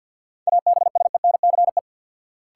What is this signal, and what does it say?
Morse code, heard as a keyed tone, about 50 wpm, 700 Hz. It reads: ABSENCE
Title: Sets of 4 Words - New Word Drill - 50wpm